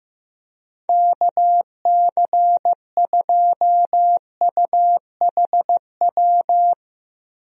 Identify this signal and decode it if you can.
Morse code, heard as a keyed tone, about 15 wpm, 700 Hz. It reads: KC2UHW
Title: USA Call Signs - 6 Characters - 15wpm